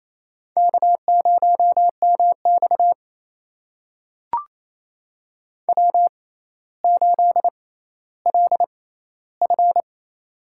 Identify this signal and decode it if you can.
Morse code, heard as a keyed tone, about 28 wpm, spaced as for 14 wpm, 700 Hz. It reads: K0MX W8LF
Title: USA Call Signs - 4 Characters - 6x Character Spacing - 28wpm